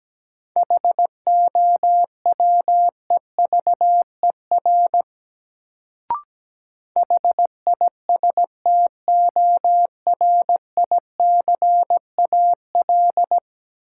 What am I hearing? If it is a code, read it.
Morse code, heard as a keyed tone, about 17 wpm, 700 Hz. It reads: HOWEVER HISTORICAL
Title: Sentences from Top 600 Words - New Word Drill - 17wpm